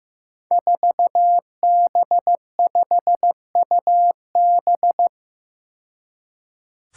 Morse code, heard as a keyed tone, about 15 wpm, 700 Hz. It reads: 4B5UB
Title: Five Challenging Letter-Number Combinations 15wpm